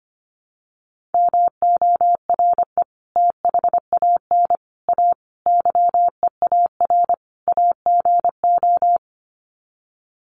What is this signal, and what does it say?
Morse code, heard as a keyed tone, about 25 wpm, 700 Hz. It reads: MORE THAN A YEAR AGO
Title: Sets of 5 Words Encore2 25wpm